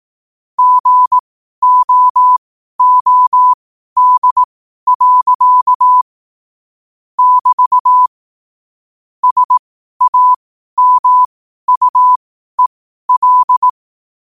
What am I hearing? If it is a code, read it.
Morse code, heard as a keyed tone, about 18 wpm, spaced as for 12 wpm, 1000 Hz. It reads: GOOD. <BT> SAMUEL
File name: qotdfiles / 12wpm